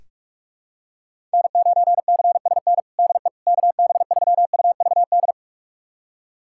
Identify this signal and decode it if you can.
Morse code, heard as a keyed tone, about 41 wpm, 700 Hz. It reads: N9CSN DE KB3UVD
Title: USA Callsign DE Callsign 45wpm